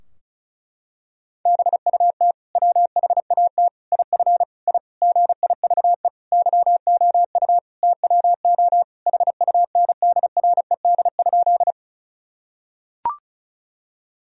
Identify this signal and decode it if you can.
Morse code, heard as a keyed tone, about 34 wpm, 700 Hz. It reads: BUT WHAT IF I GIVE YOU TWO HUNDRED?
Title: Sentences from Top 1000 Words - Review Part 2 - 35wpm